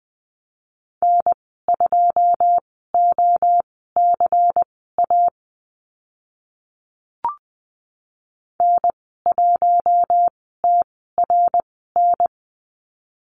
Morse code, heard as a keyed tone, about 20 wpm, spaced as for 14 wpm, 700 Hz. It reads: N2OCA N1TRN
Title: 128 Lesson 21g - Number 6 - Call Signs 20wpm